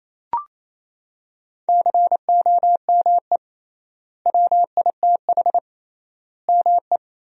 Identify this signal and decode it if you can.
Morse code, heard as a keyed tone, about 28 wpm, 700 Hz. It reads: COME WITH ME
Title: 3x Word-Spacing - Sentences from Top 200 Words - 28wpm